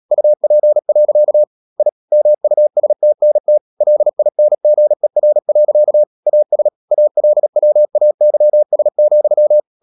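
Morse code, heard as a keyed tone, about 37 wpm, 600 Hz. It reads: UP. I MUSTNT LINGER. AS ALWAYS,
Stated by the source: Morse code practice at 35–39 wpm sending English text of Ken Follett's novel Never, from CW med Gnister.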